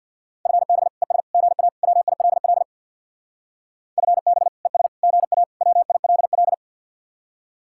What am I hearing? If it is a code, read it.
Morse code, heard as a keyed tone, about 64 wpm, 700 Hz. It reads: FB ES GA WILL FB ES GA WILL